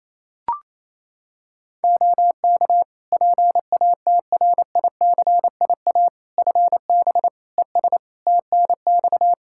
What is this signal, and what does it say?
Morse code, heard as a keyed tone, about 28 wpm, 700 Hz. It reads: OK PATRICIA FB ES TNX
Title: Practice QSO - Exchange 2 - Info - 28wpm